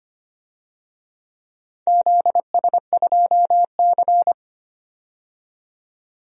Morse code, heard as a keyed tone, about 25 wpm, 700 Hz. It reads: ZS2C
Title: DXCC-100 Call Signs - 4 Characters - 25wpm